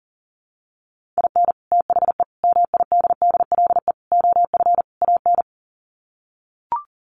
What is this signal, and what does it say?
Morse code, heard as a keyed tone, about 40 wpm, 700 Hz. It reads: IN THE MIDDLE OF AN
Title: Sets of 5 Words Encore2 40wpm